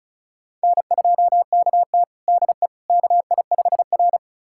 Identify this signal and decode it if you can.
Morse code, heard as a keyed tone, about 36 wpm, 700 Hz. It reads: N2KT DE KI5R
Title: USA Callsign DE Callsign - 4 Characters - 35wpm